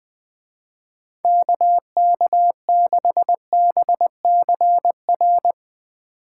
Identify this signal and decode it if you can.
Morse code, heard as a keyed tone, about 20 wpm, 700 Hz. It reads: KK6BCR
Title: Speed Racing - USA 2x3 Call Signs 17wpm